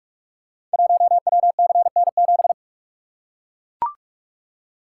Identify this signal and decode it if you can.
Morse code, heard as a keyed tone, about 44 wpm, 690 Hz. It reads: 1WCN7
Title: Code Groups - 5 Letter:Numbers - 45wpm